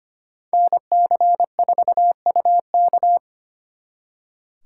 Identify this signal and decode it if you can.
Morse code, heard as a keyed tone, about 32 wpm, 700 Hz. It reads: NC4UK